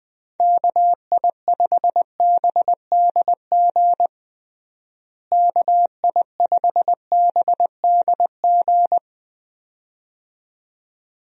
Morse code, heard as a keyed tone, about 20 wpm, 700 Hz. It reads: KI5BDG KI5BDG
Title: USA Call Signs - 6 Characters - Repeated 2x - 20wpm